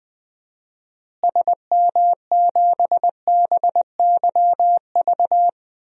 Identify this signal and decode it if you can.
Morse code, heard as a keyed tone, about 20 wpm, 700 Hz. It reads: SM7BYV
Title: DXCC-100 Call Signs - 6 Characters - Repeated 3x - 20wpm